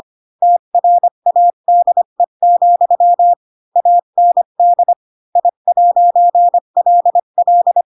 Morse code, heard as a keyed tone, about 25 wpm, 700 Hz. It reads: TRADE, AND I'LL
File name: Pinocchio / Pinocchio ch8 20 25 ik1hge